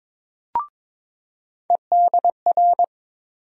Morse code, heard as a keyed tone, about 22 wpm, 700 Hz. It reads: EDR
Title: USA Call Sign Postfixes - 3 Letters 22wpm